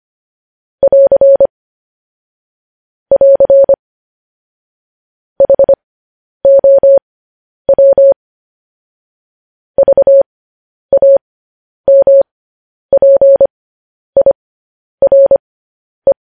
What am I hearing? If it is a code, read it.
Morse code, heard as a keyed tone, about 25 wpm, spaced as for 10 wpm, 550 Hz. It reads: <AR> <AR> HOW VAMPIRE